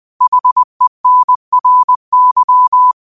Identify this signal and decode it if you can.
Morse code, heard as a keyed tone, about 20 wpm, 1000 Hz. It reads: HENRY
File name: qotdfiles / 20wpm